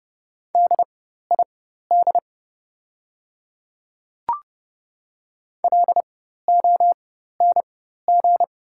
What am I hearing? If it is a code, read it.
Morse code, heard as a keyed tone, about 30 wpm, spaced as for 14 wpm, 700 Hz. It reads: DID LONG
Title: 4x Character Spacing - Top 100 Words - 30wpm